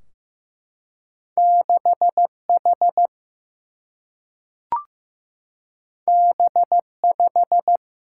Morse code, heard as a keyed tone, about 15 wpm, 700 Hz. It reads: 6H B5
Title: Single-Letter Single-Number 15wpm